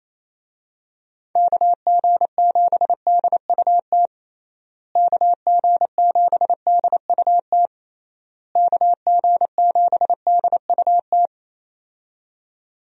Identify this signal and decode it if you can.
Morse code, heard as a keyed tone, about 28 wpm, 700 Hz. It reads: KG7DUT KG7DUT KG7DUT
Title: USA Call Signs - 6 Characters - Repeated 3x - 28wpm